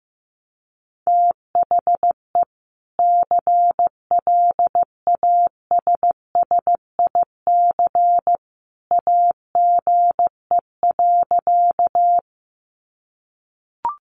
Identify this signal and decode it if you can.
Morse code, heard as a keyed tone, about 15 wpm, 700 Hz. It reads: THE CLASSIC AGE.
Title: Sentences from Top 1000 Words 15wpm